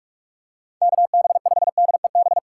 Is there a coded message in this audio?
Yes: Morse code from 0.8 to 2.4 s, heard as a keyed tone, about 46 wpm, 700 Hz. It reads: KB5DEB